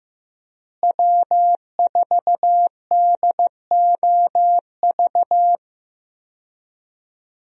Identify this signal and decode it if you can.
Morse code, heard as a keyed tone, about 15 wpm, 700 Hz. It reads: W4DOV
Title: USA Call Signs - 5 Characters - Repeated 3x - 15wpm